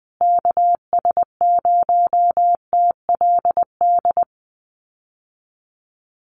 Morse code, heard as a keyed tone, about 19 wpm, 700 Hz. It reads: KS0TLD
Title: USA Call Signs Repeated 3x 20wpm